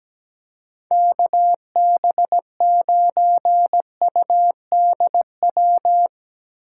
Morse code, heard as a keyed tone, about 17 wpm, 700 Hz. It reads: KB9UDW
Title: Speed Racing - USA 2x3 Call Signs 17wpm